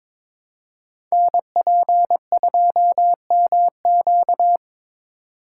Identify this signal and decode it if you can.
Morse code, heard as a keyed tone, about 21 wpm, 700 Hz. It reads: NP2MQ
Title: Code Groups - 5 Letter:Numbers - 22wpm